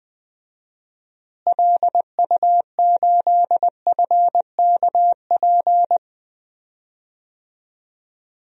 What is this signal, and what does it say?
Morse code, heard as a keyed tone, about 21 wpm, 700 Hz. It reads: LU8FKP
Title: DXCC-100 Call Signs - 6 Characters - Repeated 3x - 20wpm